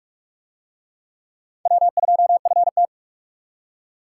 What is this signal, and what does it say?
Morse code, heard as a keyed tone, about 47 wpm, 700 Hz. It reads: W2FT